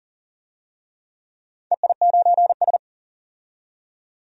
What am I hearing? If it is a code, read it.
Morse code, heard as a keyed tone, about 40 wpm, 700 Hz. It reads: EI9S
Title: DXCC-100 Call Signs - 4 Characters - 40wpm